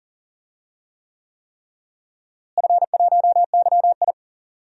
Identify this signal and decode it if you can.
Morse code, heard as a keyed tone, about 40 wpm, 700 Hz. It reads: F1YI
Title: DXCC-100 Call Signs - 4 Characters - Repeated 3x - 40wpm